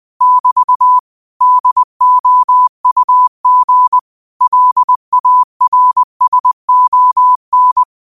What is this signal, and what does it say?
Morse code, heard as a keyed tone, about 20 wpm, 1000 Hz. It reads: <BT> DOUG LARSON